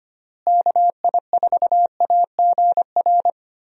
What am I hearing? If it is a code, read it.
Morse code, heard as a keyed tone, about 25 wpm, 700 Hz. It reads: KI4AGR
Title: USA Call Signs - 6 Characters - Repeated 2x - 25wpm